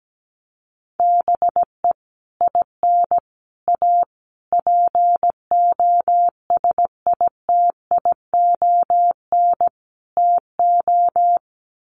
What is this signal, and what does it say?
Morse code, heard as a keyed tone, about 17 wpm, 700 Hz. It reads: BE IN A POSITION TO
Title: Sets of 5 Words Encore2 17wpm